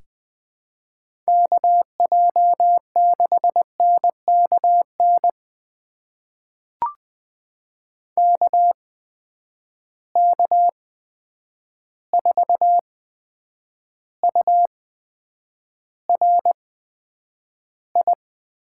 Morse code, heard as a keyed tone, about 20 wpm, spaced as for 9 wpm, 700 Hz. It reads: KJ6NKN KK4URI
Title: USA Call Signs - 6 Characters - 8x Character Spacing - 20wpm